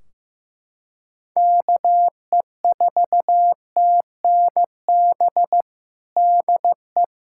Morse code, heard as a keyed tone, about 16 wpm, 700 Hz. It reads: KE4TNB DE
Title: Practice QSO - Exchange 4 - Ending - 15wpm